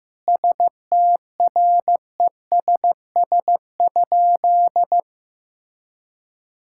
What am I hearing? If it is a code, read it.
Morse code, heard as a keyed tone, about 15 wpm, 700 Hz. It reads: STRESS?
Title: Sentences from Top 1000 Words - Review Part 6 - 15wpm